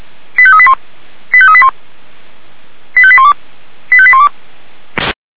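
Five-tone selective call (ZVEI), heard as quick runs of five tones, tones of about 70 ms. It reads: 85381 85381 85812 85812
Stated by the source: ZVEI five-tone selective call received on SDR.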